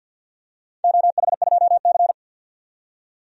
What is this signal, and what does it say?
Morse code, heard as a keyed tone, about 52 wpm, 700 Hz. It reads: OH2C
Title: DXCC-100 Call Signs - 4 Characters - 50wpm